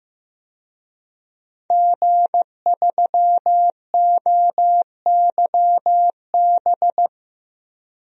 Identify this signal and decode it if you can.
Morse code, heard as a keyed tone, about 15 wpm, 700 Hz. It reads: G3OYB
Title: DXCC-100 Call Signs - 5 Characters - Repeated 2x - 15wpm